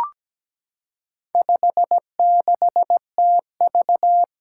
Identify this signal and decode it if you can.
Morse code, heard as a keyed tone, about 17 wpm, 700 Hz. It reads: 56TV
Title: Four Challenging Letter-Number Combinations 17wpm